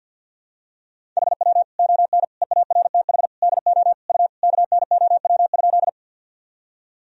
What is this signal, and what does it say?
Morse code, heard as a keyed tone, about 49 wpm, 700 Hz. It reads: HW ON EARTH DO U KNOW?
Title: Sentences from Top 200 Words and Abbreviations 50wpm